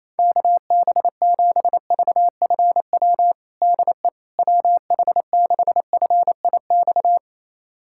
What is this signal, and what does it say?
Morse code, heard as a keyed tone, about 28 wpm, 700 Hz. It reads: KB7VFW DE WH6FIX
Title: USA Callsign DE Callsign - 6 Characters - 28wpm